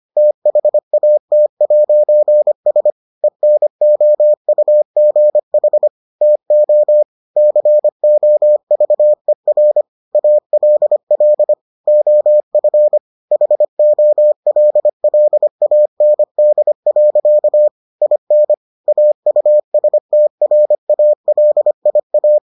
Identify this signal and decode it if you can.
Morse code, heard as a keyed tone, about 25 wpm, 600 Hz. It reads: THAT'S ENOUGH TO COVER ALL OF HOLLAND. IN AUSTRALIA